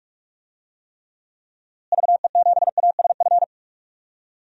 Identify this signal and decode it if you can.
Morse code, heard as a keyed tone, about 46 wpm, 700 Hz. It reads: VE7ASF
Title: DXCC-100 Call Signs - 6 Characters - Repeated 3x - 45wpm